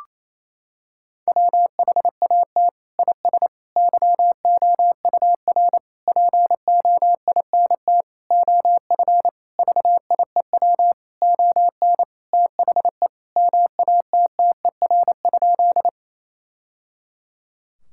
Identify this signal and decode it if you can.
Morse code, heard as a keyed tone, about 27 wpm, 700 Hz. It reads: WHAT IS YOUR POINT OF VIEW ON THE MATTER?
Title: Sentences from Top 1000 Words - Review Part 6 - 28wpm